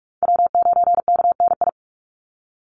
Morse code, heard as a keyed tone, about 44 wpm, 700 Hz. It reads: W9KNI
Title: Call Signs - 45wpm